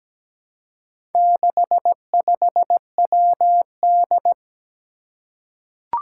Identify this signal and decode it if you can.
Morse code, heard as a keyed tone, about 17 wpm, 700 Hz. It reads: 65WD
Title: Four Challenging Letter-Number Combinations 17wpm